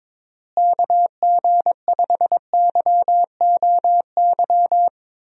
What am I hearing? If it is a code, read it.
Morse code, heard as a keyed tone, about 23 wpm, 700 Hz. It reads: KG5YOY
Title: USA Call Signs - 6 Characters - Repeated 3x - 22wpm